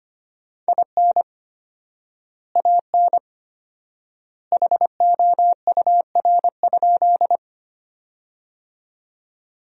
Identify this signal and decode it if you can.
Morse code, heard as a keyed tone, about 25 wpm, 700 Hz. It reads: IN AN HOUR?
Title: 4x Word-Spacing - Sentences from Top 400 Words - 25wpm